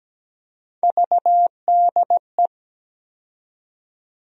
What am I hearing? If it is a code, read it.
Morse code, heard as a keyed tone, about 17 wpm, 700 Hz. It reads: VDE
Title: USA Call Sign Postfixes - 3 Letters 17wpm